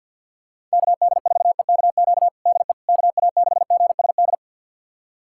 Morse code, heard as a keyed tone, about 53 wpm, 700 Hz. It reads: KD4EKQ DE KA6GSD
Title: USA Callsign DE Callsign - 6 Characters - 50wpm